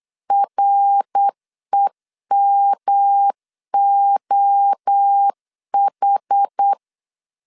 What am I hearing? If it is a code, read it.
Morse code, heard as a keyed tone, about 8 wpm, 800 Hz. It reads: REMOH